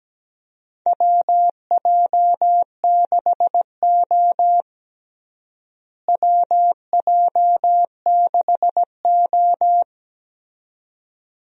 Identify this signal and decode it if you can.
Morse code, heard as a keyed tone, about 17 wpm, 700 Hz. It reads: WJ6O WJ6O
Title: USA Call Signs - 4 Characters - Repeated 2x - 17wpm